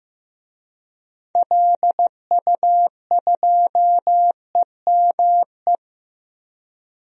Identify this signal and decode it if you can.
Morse code, heard as a keyed tone, about 15 wpm, 700 Hz. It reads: LU2EME
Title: DXCC-100 Call Signs - 6 Characters - 15wpm